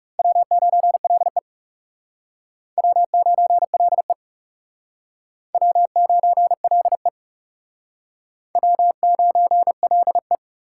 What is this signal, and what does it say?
Morse code, heard as a keyed tone, about 36 wpm, 700 Hz. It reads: W9LE W9LE W9LE W9LE
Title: Speed Racing - USA Call Signs 30wpm